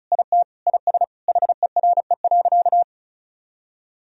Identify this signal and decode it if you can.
Morse code, heard as a keyed tone, about 36 wpm, 700 Hz. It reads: IT IS HERE.
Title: Sentences from Top 1000 Words - Review Part 2 - 35wpm